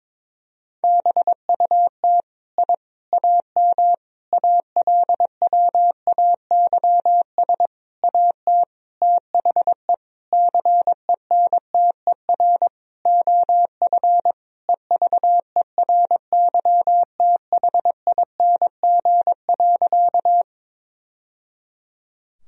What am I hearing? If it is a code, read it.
Morse code, heard as a keyed tone, about 22 wpm, 700 Hz. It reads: BUT I AM ALWAYS AT THE CENTER OF EVERYTHING.
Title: Sentences from Top 1000 Words - Review Part 2 - 22wpm